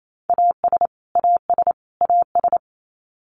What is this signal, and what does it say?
Morse code, heard as a keyed tone, about 28 wpm, 700 Hz. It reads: AS AS AS
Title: Top 200 Words Repeated 28wpm